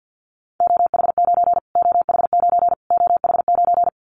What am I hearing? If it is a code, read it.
Morse code, heard as a keyed tone, about 50 wpm, 700 Hz. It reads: OH9 OH9 OH9
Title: Callsign Prefixes for Top 25 DXCC - Repeated 3x - 50wpm